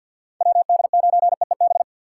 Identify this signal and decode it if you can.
Morse code, heard as a keyed tone, about 54 wpm, 700 Hz. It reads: WD9EEB
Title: USA Call Signs - 6 Characters - 6x Character Spacing - 50wpm